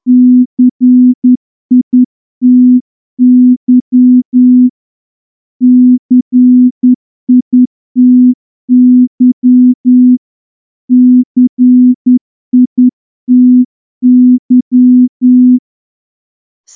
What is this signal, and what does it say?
Morse code, heard as a keyed tone, about 10 wpm, 250 Hz. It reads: CITY CITY CITY
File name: morse-training-10WPM